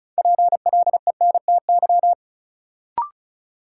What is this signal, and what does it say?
Morse code, heard as a keyed tone, about 34 wpm, 700 Hz. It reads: PLENTY
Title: Sentences from Top 1300 Words - New Word Drill Alt - 35wpm